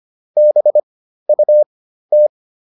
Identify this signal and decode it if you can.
Morse code, heard as a keyed tone, about 25 wpm, spaced as for 13 wpm, 600 Hz. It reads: BUT